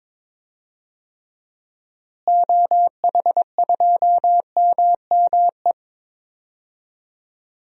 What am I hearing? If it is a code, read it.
Morse code, heard as a keyed tone, about 22 wpm, 700 Hz. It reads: OH2MME
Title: DXCC-100 Call Signs - 6 Characters - Repeated 3x - 22wpm